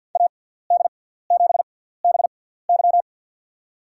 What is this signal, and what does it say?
Morse code, heard as a keyed tone, about 51 wpm, spaced as for 17 wpm, 700 Hz. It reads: AD7BY
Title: USA Call Signs - 5 Characters - 6x Character Spacing - 50wpm